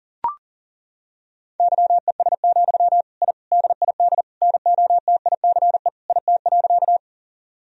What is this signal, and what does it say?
Morse code, heard as a keyed tone, about 37 wpm, 700 Hz. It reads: YES, I DID NOTICE IT.
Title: Sentences from Top 1000 Words - Review Part 2 - 40wpm